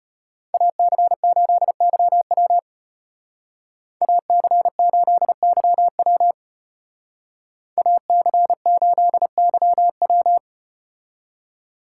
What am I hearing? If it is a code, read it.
Morse code, heard as a keyed tone, about 33 wpm, 700 Hz. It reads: AC8YW AC8YW AC8YW